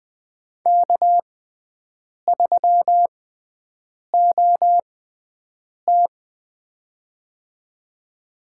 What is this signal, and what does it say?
Morse code, heard as a keyed tone, about 20 wpm, spaced as for 7 wpm, 700 Hz. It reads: K3OT
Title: USA Call Signs - 4 Characters - 6x Character Spacing - 20wpm